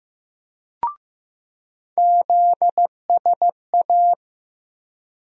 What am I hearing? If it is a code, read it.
Morse code, heard as a keyed tone, about 15 wpm, 700 Hz. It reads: ZSA
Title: USA Call Sign Postfixes - 3 Letters 15wpm